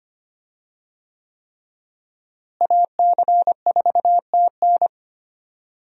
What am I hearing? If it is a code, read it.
Morse code, heard as a keyed tone, about 25 wpm, 700 Hz. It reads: AC4TN
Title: USA Call Signs - 5 Characters - Repeated 3x - 25wpm